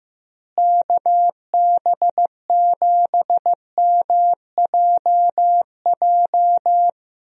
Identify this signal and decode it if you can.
Morse code, heard as a keyed tone, about 15 wpm, 700 Hz. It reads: KB7MJJ
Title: USA Call Signs - 6 Characters - Repeated 3x - 15wpm